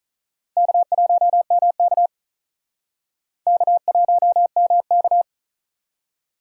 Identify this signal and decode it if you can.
Morse code, heard as a keyed tone, about 45 wpm, 700 Hz. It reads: K1MK K1MK